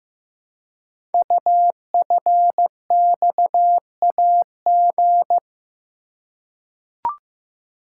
Morse code, heard as a keyed tone, about 15 wpm, 700 Hz. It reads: UFXAG
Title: Code Groups - 5 Letters - 15wpm